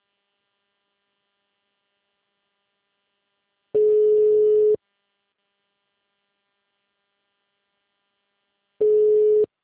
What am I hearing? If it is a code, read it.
Ringback tone (Europe, 425 Hz).
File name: out- -2010- - - .16001